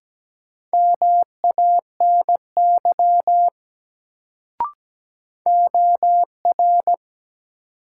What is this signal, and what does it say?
Morse code, heard as a keyed tone, about 17 wpm, 700 Hz. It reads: MANY OR
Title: Sentences from Top 100 Words - New Word Drill - 17wpm